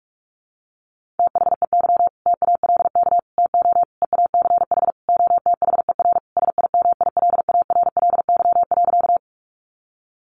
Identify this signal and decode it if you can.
Morse code, heard as a keyed tone, about 44 wpm, 700 Hz. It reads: THEY TALK TO EACH OTHER SIMILARLY.
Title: Sentences from Top 1400 Words 45wpm